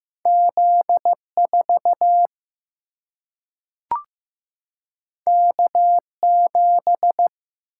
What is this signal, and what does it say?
Morse code, heard as a keyed tone, about 15 wpm, 700 Hz. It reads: Z4 K7
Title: Single-Letter Single-Number 15wpm